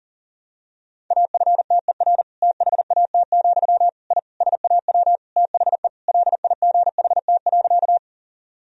Morse code, heard as a keyed tone, about 40 wpm, 700 Hz. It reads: AFTER THAT, I SAW THE LIGHT.
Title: Sentences from Top 1000 Words - Review Part 1 - 40wpm